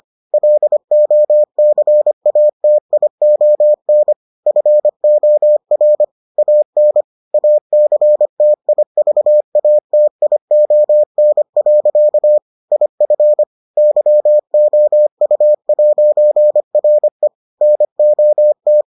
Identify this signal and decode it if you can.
Morse code, heard as a keyed tone, about 25 wpm, 600 Hz. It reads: LOCATION FOR AN ACTIVATION. IF YOU'RE NOT